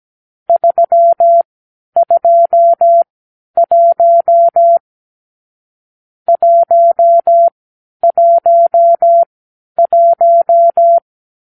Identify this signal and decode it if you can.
Morse code, heard as a keyed tone, about 17 wpm, spaced as for 10 wpm, 680 Hz. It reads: 321 111
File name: k7qo / Lesson 44 (1-3)